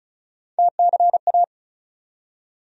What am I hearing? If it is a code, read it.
Morse code, heard as a keyed tone, about 34 wpm, 700 Hz. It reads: TCA
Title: USA Call Sign Postfixes - 3 Letters 35wpm